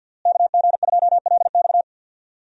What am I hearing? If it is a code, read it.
Morse code, heard as a keyed tone, about 52 wpm, 700 Hz. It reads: KG2LX
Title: USA Call Signs - 5 Characters - Repeated 3x - 50wpm